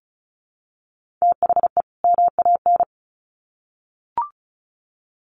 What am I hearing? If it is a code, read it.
Morse code, heard as a keyed tone, about 34 wpm, 700 Hz. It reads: THE MAN
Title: Sets of 2 Words 35wpm